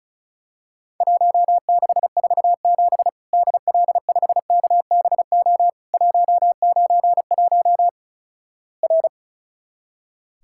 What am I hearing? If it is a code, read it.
Morse code, heard as a keyed tone, about 36 wpm, 700 Hz. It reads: 1647 DL5KBO 191 R
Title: Worked All Europe DX Contest - QTC Exchange - 35wpm